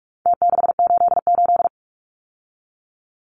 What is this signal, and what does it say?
Morse code, heard as a keyed tone, about 44 wpm, 700 Hz. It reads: T688
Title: Numbers - Contest Style - 4-Digit 45wpm